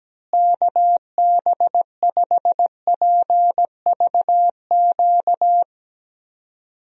Morse code, heard as a keyed tone, about 17 wpm, 700 Hz. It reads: KB5PVQ
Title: USA Call Signs - 6 Characters - Repeated 3x - 17wpm